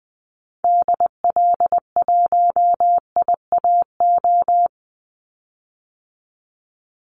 Morse code, heard as a keyed tone, about 20 wpm, 700 Hz. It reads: DL1IAO
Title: Call Signs Repeated 2x - 20wpm